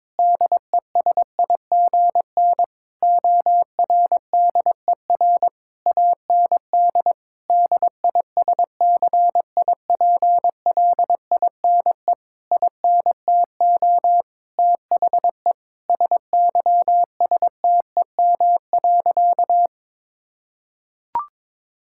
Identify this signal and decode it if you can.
Morse code, heard as a keyed tone, about 22 wpm, 700 Hz. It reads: DESIGN ORDER AND DISCIPLINE INTO THE SYSTEM.
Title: Sentences from Top 1000 Words - Review Part 6 - 22wpm